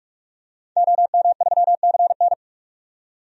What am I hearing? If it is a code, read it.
Morse code, heard as a keyed tone, about 45 wpm, 700 Hz. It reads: OM3CN
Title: DXCC-100 Call Signs - 5 Characters - 45wpm